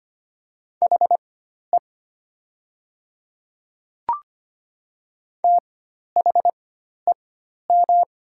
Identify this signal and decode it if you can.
Morse code, heard as a keyed tone, about 25 wpm, spaced as for 12 wpm, 700 Hz. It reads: HE THEM
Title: 4x Character Spacing - Top 100 Words - 25wpm